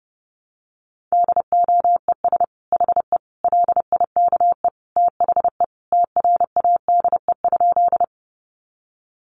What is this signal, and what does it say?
Morse code, heard as a keyed tone, about 30 wpm, 700 Hz. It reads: DOES HE LIKE THE TRADE?
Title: Sentences from Top 600 Words 30wpm